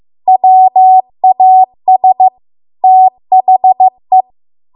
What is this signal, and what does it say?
Morse code, heard as a keyed tone, about 15 wpm, 750 Hz. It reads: WAS THE
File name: Archive / 15 WPM / 15WPM